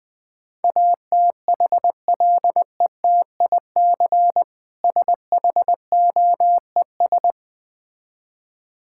Morse code, heard as a keyed tone, about 20 wpm, 700 Hz. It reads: ATHLETIC SHOES
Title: Easy Words - Speed-Racing with Hints - 20wpm